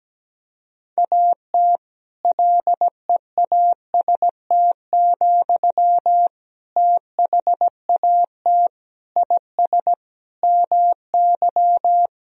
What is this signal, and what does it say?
Morse code, heard as a keyed tone, about 17 wpm, 700 Hz. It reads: AT LEAST, THAT IS MY
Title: Sentences from Top 1000 Words - Review Part 3 - 17wpm